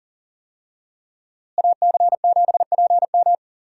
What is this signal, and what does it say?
Morse code, heard as a keyed tone, about 37 wpm, 700 Hz. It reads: AC7PM
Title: USA Call Signs - 5 Characters - Repeated 2x - 40wpm